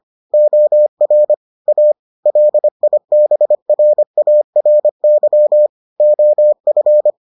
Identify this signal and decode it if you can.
Morse code, heard as a keyed tone, about 25 wpm, 600 Hz. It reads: OR A LIBRARY OF